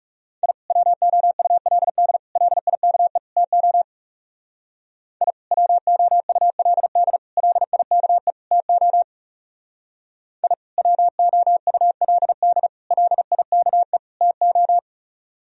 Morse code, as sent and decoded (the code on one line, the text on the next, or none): .. / .-- --- ..- .-.. -.. / .-.. .. -.- . / - --- / .. / .-- --- ..- .-.. -.. / .-.. .. -.- . / - --- / .. / .-- --- ..- .-.. -.. / .-.. .. -.- . / - ---
I WOULD LIKE TO I WOULD LIKE TO I WOULD LIKE TO